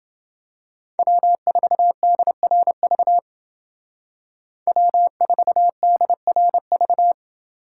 Morse code, heard as a keyed tone, about 28 wpm, 700 Hz. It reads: W4DRV W4DRV